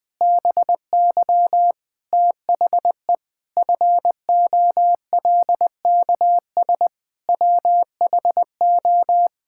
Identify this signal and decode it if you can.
Morse code, heard as a keyed tone, about 20 wpm, 700 Hz. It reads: BY THE FOLKS WHO
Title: Pangram Challenge v2 - 20wpm